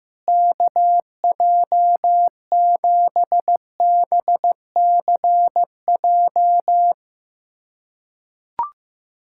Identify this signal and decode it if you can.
Morse code, heard as a keyed tone, about 16 wpm, 700 Hz. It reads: KJ7BCJ
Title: USA Callsign DE Callsign 15wpm